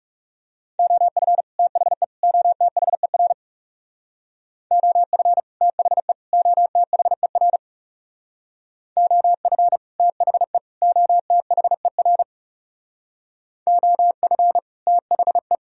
Morse code, heard as a keyed tone, about 37 wpm, 700 Hz. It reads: OF THE OTHER OF THE OTHER OF THE OTHER OF THE
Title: Speed-Racing - Sets of 3 Words 30wpm